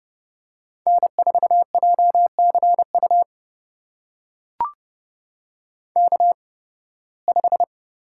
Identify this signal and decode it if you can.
Morse code, heard as a keyed tone, about 29 wpm, spaced as for 19 wpm, 700 Hz. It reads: N4JCU K5
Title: USA Call Signs - 5 Characters - 8x Character Spacing - 30wpm